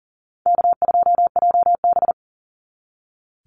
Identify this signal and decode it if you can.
Morse code, heard as a keyed tone, about 40 wpm, 700 Hz. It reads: K2JB
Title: NA SOTA Chaser Callsigns 40wpm